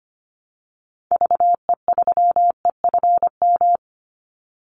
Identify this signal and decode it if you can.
Morse code, heard as a keyed tone, about 25 wpm, 700 Hz. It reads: VE3EFM
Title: Call Signs - 25wpm